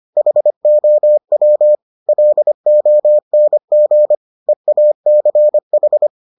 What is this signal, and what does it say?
Morse code, heard as a keyed tone, about 25 wpm, 600 Hz. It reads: HOW LONG EACH